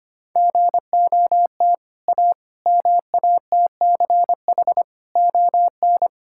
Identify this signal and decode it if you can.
Morse code, heard as a keyed tone, about 25 wpm, 700 Hz. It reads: GOT A MATCH ON
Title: Sentences from Top 1600 Words 25wpm